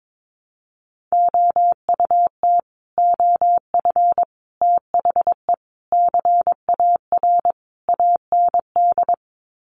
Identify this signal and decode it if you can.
Morse code, heard as a keyed tone, about 22 wpm, 700 Hz. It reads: OUT OF THE CAR AND
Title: Sets of 5 Words Encore 22wpm